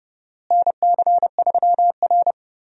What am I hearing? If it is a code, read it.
Morse code, heard as a keyed tone, about 30 wpm, 700 Hz. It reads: NC3R
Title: USA Call Signs - 4 Characters - Repeated 2x - 30wpm